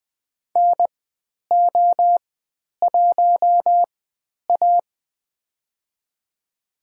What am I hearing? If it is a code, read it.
Morse code, heard as a keyed tone, about 20 wpm, spaced as for 10 wpm, 700 Hz. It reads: NO1A
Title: 095 Lesson 16g - Letter c - Call Signs 20wpm